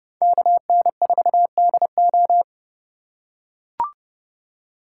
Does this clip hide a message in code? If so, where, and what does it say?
Yes: Morse code from 0.2 to 2.4 s, heard as a keyed tone, about 29 wpm, 700 Hz. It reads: KN4DO